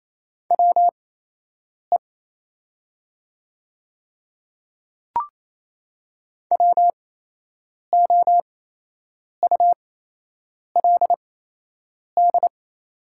Morse code, heard as a keyed tone, about 28 wpm, spaced as for 8 wpm, 700 Hz. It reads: WE WOULD
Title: 8x Character Spacing - Top 100 Words - 28wpm